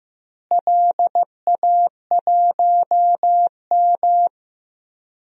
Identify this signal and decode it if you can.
Morse code, heard as a keyed tone, about 15 wpm, 700 Hz. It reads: LA1M
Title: DXCC-100 Call Signs - 4 Characters - Repeated 3x - 15wpm